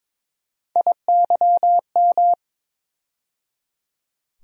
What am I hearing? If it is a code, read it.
Morse code, heard as a keyed tone, about 22 wpm, 700 Hz. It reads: IYM
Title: USA Call Sign Postfixes - 3 Letters 22wpm